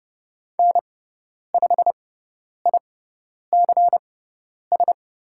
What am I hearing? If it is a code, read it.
Morse code, heard as a keyed tone, about 30 wpm, spaced as for 10 wpm, 700 Hz. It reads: N5ICS
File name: Lesson 17g - Letter u - Call Signs 30wpm